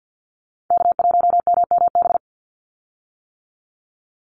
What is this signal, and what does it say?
Morse code, heard as a keyed tone, about 50 wpm, 700 Hz. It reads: K1MMB
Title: Call Signs - 50wpm